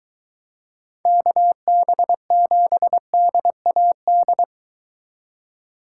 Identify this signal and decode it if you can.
Morse code, heard as a keyed tone, about 22 wpm, 700 Hz. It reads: KB7DAD